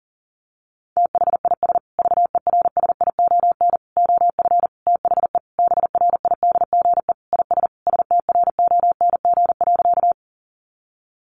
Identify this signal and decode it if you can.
Morse code, heard as a keyed tone, about 40 wpm, 700 Hz. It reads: THIS VERSION OF THE BRIDGE IS STRONG.
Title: Sentences from Top 1400 Words 40wpm